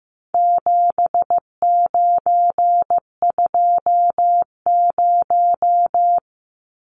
Morse code, heard as a keyed tone, about 15 wpm, 700 Hz. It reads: 7920
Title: Numbers - 4-Digit 15wpm